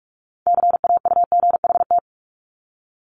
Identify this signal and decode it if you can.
Morse code, heard as a keyed tone, about 46 wpm, 700 Hz. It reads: CAUGHT
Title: Multi-letter Phonemes Repeated 3x 45wpm